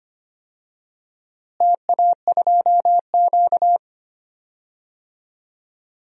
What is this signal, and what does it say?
Morse code, heard as a keyed tone, about 25 wpm, 700 Hz. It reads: TA2Q
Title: DXCC-100 Call Signs - 4 Characters - Repeated 2x - 25wpm